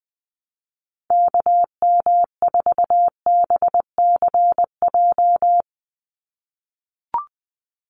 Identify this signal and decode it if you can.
Morse code, heard as a keyed tone, about 20 wpm, 700 Hz. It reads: KM4BCJ
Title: USA Call Signs 20wpm